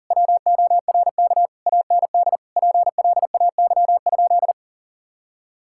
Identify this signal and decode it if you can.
Morse code, heard as a keyed tone, about 40 wpm, 690 Hz. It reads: WORK AND PLAY?
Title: Sentences from Top 1000 Words - Review Part 6 - 40wpm